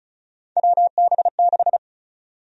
Morse code, heard as a keyed tone, about 36 wpm, 700 Hz. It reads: WB6
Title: USA Call Sign Prefixes - Repeated 2x - 35wpm